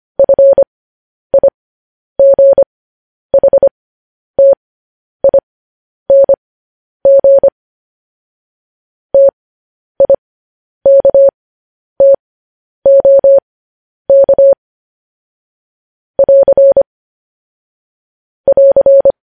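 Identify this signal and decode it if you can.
Morse code, heard as a keyed tone, about 25 wpm, spaced as for 10 wpm, 550 Hz. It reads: FIGHTING TIKTOK <AR> <AR>
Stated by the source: Morse code news headlines transmission.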